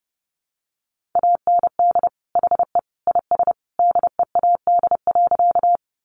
Morse code, heard as a keyed tone, about 30 wpm, 700 Hz. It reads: AND HE IS DEAD.
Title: Sentences from Top 1500 Words 30wpm